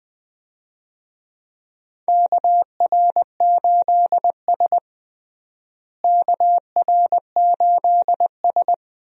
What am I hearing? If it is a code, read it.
Morse code, heard as a keyed tone, about 20 wpm, 700 Hz. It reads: KR8S KR8S
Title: USA Call Signs - 4 Characters - Repeated 3x - 20wpm